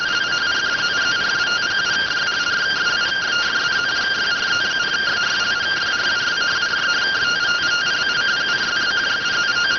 Two-tone teleprinter signal, about 170 Hz shift.